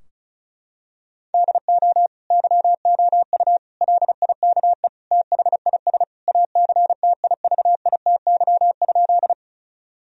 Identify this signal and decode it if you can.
Morse code, heard as a keyed tone, about 36 wpm, 700 Hz. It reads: DO YOU LIKE THIS ACTIVITY?
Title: Sentences from Top 1000 Words - Review Part 4 - 35wpm